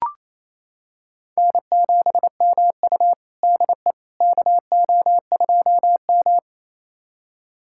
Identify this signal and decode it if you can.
Morse code, heard as a keyed tone, about 28 wpm, 700 Hz. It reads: N7MU DE KO2M
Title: USA Callsign DE Callsign - 4 Characters - 28wpm